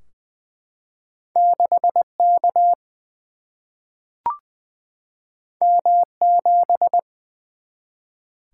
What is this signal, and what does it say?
Morse code, heard as a keyed tone, about 20 wpm, 700 Hz. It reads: 6 K M 7
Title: Single-Letter Single-Number 20wpm